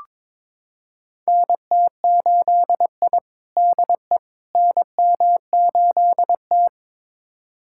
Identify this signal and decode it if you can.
Morse code, heard as a keyed tone, about 22 wpm, 700 Hz. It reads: NT8I DE NM8T
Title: USA Callsign DE Callsign - 4 Characters - 22wpm